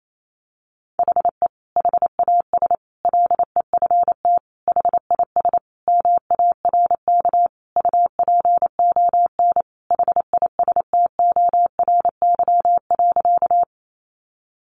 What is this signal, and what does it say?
Morse code, heard as a keyed tone, about 28 wpm, 700 Hz. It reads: HE HAS LEFT HIS MARK UPON HISTORY.
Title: Sentences from Top 1100 Words 28wpm